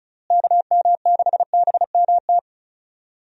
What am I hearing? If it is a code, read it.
Morse code, heard as a keyed tone, about 36 wpm, 700 Hz. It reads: KM6BMT
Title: USA Call Signs - 6 Characters - Repeated 3x - 35wpm